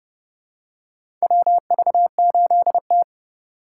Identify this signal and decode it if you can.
Morse code, heard as a keyed tone, about 32 wpm, 700 Hz. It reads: WV8T
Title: USA Call Signs - 3x Character Spacing - 30wpm